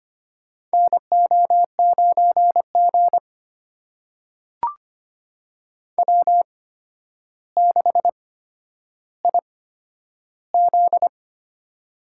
Morse code, heard as a keyed tone, about 25 wpm, spaced as for 11 wpm, 700 Hz. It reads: NO9G W6IZ
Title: USA Call Signs - 8x Character Spacing - 25wpm